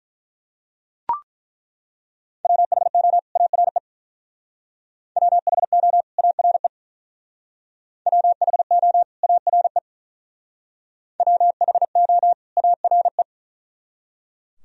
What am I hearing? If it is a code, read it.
Morse code, heard as a keyed tone, about 41 wpm, 700 Hz. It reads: WHO ARE WHO ARE WHO ARE WHO ARE
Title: Speed-Racing - Sets of 2 Words 35wpm